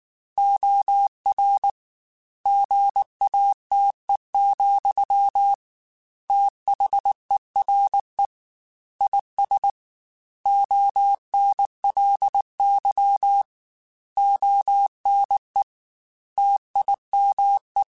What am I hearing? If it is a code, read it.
Morse code, heard as a keyed tone, about 19 wpm, 800 Hz. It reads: OR GATE, THERE IS ONLY ONE TIME
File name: Technical Subjects / Technical Subjects-mp3s / OR Gate @18